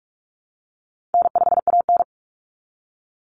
Morse code, heard as a keyed tone, about 44 wpm, 700 Hz. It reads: N5AN
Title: Call Signs Repeated 3x - 45wpm